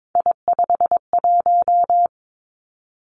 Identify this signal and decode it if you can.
Morse code, heard as a keyed tone, about 22 wpm, 700 Hz. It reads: I51
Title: Three Challenging Letter-Number Combinations 22wpm